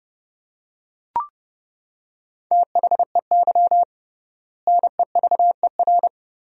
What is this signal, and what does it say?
Morse code, heard as a keyed tone, about 30 wpm, 700 Hz. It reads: THEY NEVER
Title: 3x Word-Spacing - Sentences from Top 300 Words - 30wpm